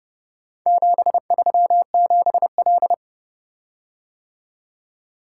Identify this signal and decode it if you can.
Morse code, heard as a keyed tone, about 30 wpm, 700 Hz. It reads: 737L